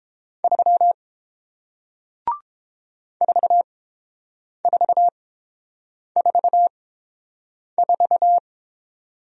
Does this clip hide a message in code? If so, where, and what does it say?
Yes: Morse code from 0.4 to 8.4 s, heard as a keyed tone, about 30 wpm, spaced as for 7 wpm, 700 Hz. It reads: 3 4444